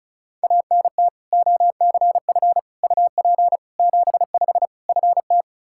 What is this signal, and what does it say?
Morse code, heard as a keyed tone, about 36 wpm, 700 Hz. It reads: ANT OCF UP 75 FT
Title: Practice QSO - Exchange 2 - Info - 35wpm